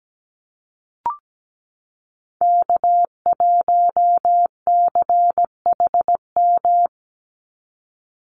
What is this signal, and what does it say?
Morse code, heard as a keyed tone, about 17 wpm, 700 Hz. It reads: K1CHM
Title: Call Signs Repeated 3x - 17wpm